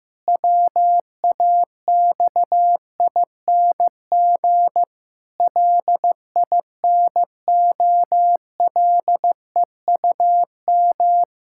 Morse code, heard as a keyed tone, about 15 wpm, 700 Hz. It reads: WAXING LINOLEUM
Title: Pangram Challenge v2 - 15wpm